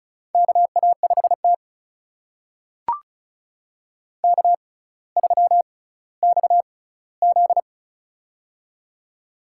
Morse code, heard as a keyed tone, about 36 wpm, spaced as for 18 wpm, 700 Hz. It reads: KA5T K3XZ
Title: USA Call Signs - 4 Characters - 6x Character Spacing - 35wpm